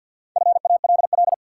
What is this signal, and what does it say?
Morse code, heard as a keyed tone, about 51 wpm, 700 Hz. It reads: FALL
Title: Four Letter Words - 50wpm